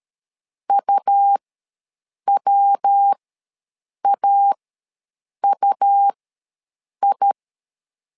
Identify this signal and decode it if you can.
Morse code, heard as a keyed tone, about 13 wpm, spaced as for 7 wpm, 800 Hz. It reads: UWAUI
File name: LEZIONI / 3 MWRO / 3 MW ws 12wpm 1816